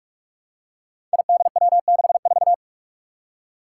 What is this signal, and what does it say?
Morse code, heard as a keyed tone, about 44 wpm, 700 Hz. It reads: IDW64
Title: Five Challenging Letter-Number Combinations 45wpm